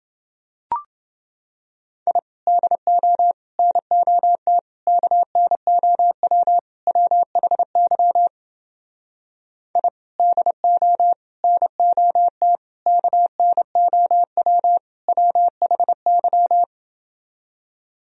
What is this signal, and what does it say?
Morse code, heard as a keyed tone, about 28 wpm, 700 Hz. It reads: I DO NOT KNOW WHY I DO NOT KNOW WHY